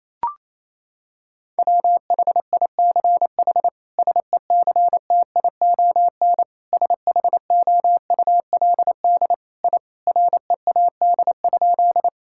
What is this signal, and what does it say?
Morse code, heard as a keyed tone, about 27 wpm, 700 Hz. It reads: WHICH SECTION SHOULD I READ?
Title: Sentences from Top 1000 Words - Review Part 5 - 28wpm